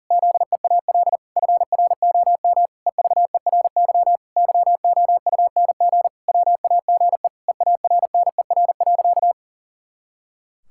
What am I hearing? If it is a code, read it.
Morse code, heard as a keyed tone, about 37 wpm, 700 Hz. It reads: ZEAL FROM EVERY YOUNG WAGE EARNER.
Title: Pangram Challenge v2 - 40wpm